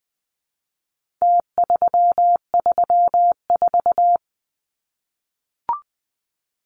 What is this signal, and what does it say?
Morse code, heard as a keyed tone, about 20 wpm, 700 Hz. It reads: T334
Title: Numbers - Contest Style - 4-Digit 20wpm